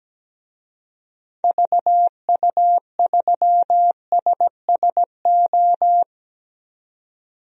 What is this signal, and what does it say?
Morse code, heard as a keyed tone, about 17 wpm, 700 Hz. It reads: VU3SSO
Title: DXCC-100 Call Signs - 6 Characters - Repeated 2x - 17wpm